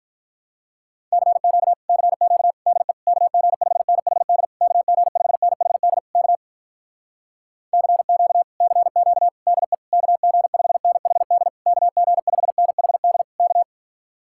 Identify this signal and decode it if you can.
Morse code, heard as a keyed tone, about 48 wpm, 700 Hz. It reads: CQ CQ DE KG5NHD KG5NHD K CQ CQ DE KG5NHD KG5NHD K